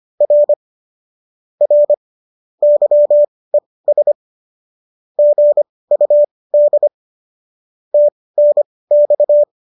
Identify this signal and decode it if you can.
Morse code, heard as a keyed tone, about 25 wpm, spaced as for 18 wpm, 590 Hz. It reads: R R YES GUD TNX
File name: qso218 18